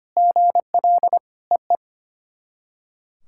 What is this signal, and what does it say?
Morse code, heard as a keyed tone, about 25 wpm, 700 Hz. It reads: GL EE
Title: SOTA QSO Practice 25wpm